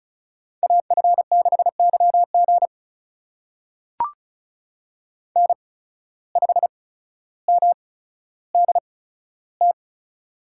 Morse code, heard as a keyed tone, about 36 wpm, spaced as for 15 wpm, 700 Hz. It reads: AF6YG N5MDT
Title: USA Call Signs - 5 Characters - 8x Character Spacing - 35wpm